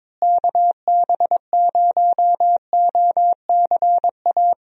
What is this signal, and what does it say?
Morse code, heard as a keyed tone, about 22 wpm, 700 Hz. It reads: KB0OCA
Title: USA Call Signs - 6 Characters - Repeated 2x - 22wpm